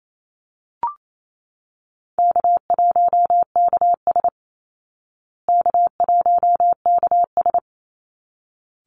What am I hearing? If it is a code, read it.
Morse code, heard as a keyed tone, about 28 wpm, 700 Hz. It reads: K1KS K1KS
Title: Easy Call Signs Repeated 2x - 28wpm